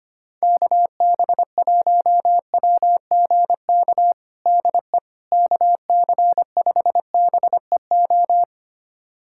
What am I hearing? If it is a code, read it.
Morse code, heard as a keyed tone, about 25 wpm, 700 Hz. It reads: KB1WGK DE KC5BEO
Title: USA Callsign DE Callsign - 6 Characters - 25wpm